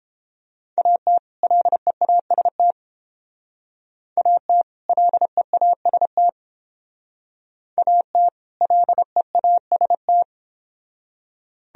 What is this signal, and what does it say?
Morse code, heard as a keyed tone, about 29 wpm, 700 Hz. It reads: AT LEAST AT LEAST AT LEAST